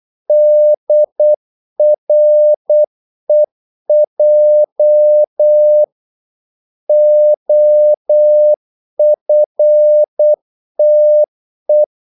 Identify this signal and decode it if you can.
Morse code, heard as a keyed tone, about 8 wpm, 600 Hz. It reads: DREJ OFTE
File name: Korte ord DK 8wpm